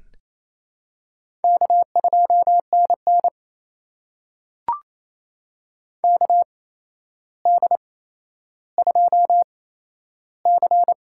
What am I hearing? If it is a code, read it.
Morse code, heard as a keyed tone, about 28 wpm, spaced as for 12 wpm, 700 Hz. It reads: K2NN KD2C
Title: USA Call Signs - 4 Characters - 8x Character Spacing - 28wpm